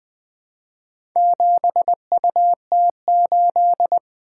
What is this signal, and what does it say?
Morse code, heard as a keyed tone, about 20 wpm, 700 Hz. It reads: 7UT8
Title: Four Letter-Number Combinations - 20wpm